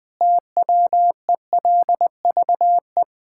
Morse code, heard as a keyed tone, about 20 wpm, 700 Hz. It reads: TWELVE
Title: Pangram Challenge v2 - 20wpm